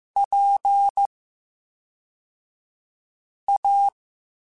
Morse code, heard as a keyed tone, about 15 wpm, spaced as for 3 wpm, 800 Hz. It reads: PA
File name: Special INGs / Special INGs-mp3s / VerySlowToFaster Ing @10